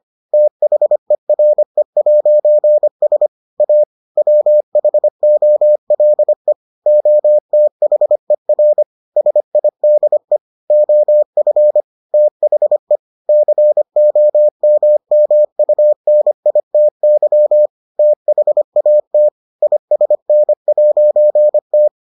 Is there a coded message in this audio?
Yes: Morse code from 0.3 to 21.9 s, heard as a keyed tone, about 25 wpm, 600 Hz. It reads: THERE'S A WHOLE OTHER SIDE OF THE COMMUNITY THAT ISN'T